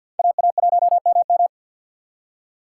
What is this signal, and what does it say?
Morse code, heard as a keyed tone, about 67 wpm, spaced as for 59 wpm, 700 Hz. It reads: AA1MM
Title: Speed Racing - USA 2x2 Call Signs 50wpm